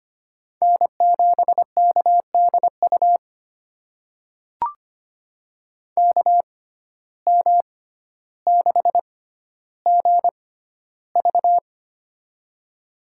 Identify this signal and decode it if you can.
Morse code, heard as a keyed tone, about 25 wpm, spaced as for 13 wpm, 700 Hz. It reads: N7KDU KM6GV
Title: USA Call Signs - 5 Characters - 6x Character Spacing - 25wpm